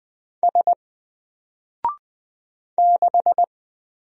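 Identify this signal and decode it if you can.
Morse code, heard as a keyed tone, about 20 wpm, 700 Hz. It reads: S 6
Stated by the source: Single letter and number Morse drill at 20 wpm, from N4LCD.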